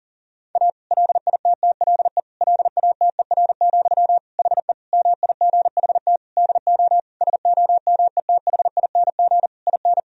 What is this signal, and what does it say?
Morse code, heard as a keyed tone, about 37 wpm, 700 Hz. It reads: A LITTLE LATER, HE MIGHT DO SOMETHING IN
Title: Sentences from Top 1000 Words - Review Part 4 - 40wpm